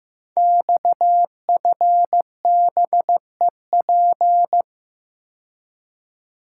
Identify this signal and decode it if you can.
Morse code, heard as a keyed tone, about 15 wpm, 700 Hz. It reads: XFBEP
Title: Code Groups - 5 Letter:Numbers - 15wpm